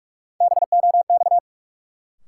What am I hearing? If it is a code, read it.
Morse code, heard as a keyed tone, about 46 wpm, 700 Hz. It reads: BOX
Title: Sentences from Top 500 Words - New Word Drill - 45wpm